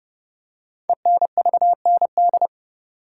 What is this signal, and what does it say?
Morse code, heard as a keyed tone, about 30 wpm, 700 Hz. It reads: ENVND
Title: Code Groups - 5 Letters - 30wpm